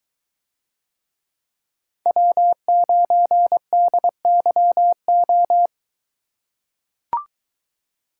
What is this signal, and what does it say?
Morse code, heard as a keyed tone, about 22 wpm, 700 Hz. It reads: W9DYO